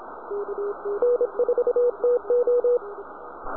Morse code, heard as a keyed tone, about 28 wpm, 490 Hz, with radio noise. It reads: N4TO